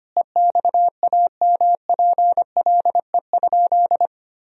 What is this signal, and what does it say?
Morse code, heard as a keyed tone, about 25 wpm, 700 Hz. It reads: EXAMPLE?
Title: Sentences from Top 1000 Words - Review Part 1 - 25wpm